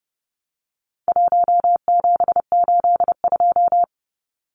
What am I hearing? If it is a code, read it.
Morse code, heard as a keyed tone, about 29 wpm, 700 Hz. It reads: 1782
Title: Numbers - 4-Digit 30wpm